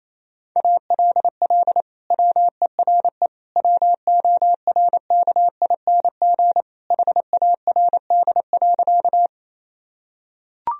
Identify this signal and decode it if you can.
Morse code, heard as a keyed tone, about 28 wpm, 700 Hz. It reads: ALL WERE WORKING HARD.
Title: Sentences from Top 1000 Words - Review Part 4 - 28wpm